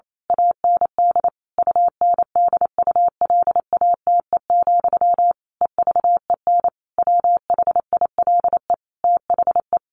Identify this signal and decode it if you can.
Morse code, heard as a keyed tone, about 28 wpm, 700 Hz. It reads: AND UNDULATE, EVEN WHILE THE
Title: A Sound Of Thunder 28wpm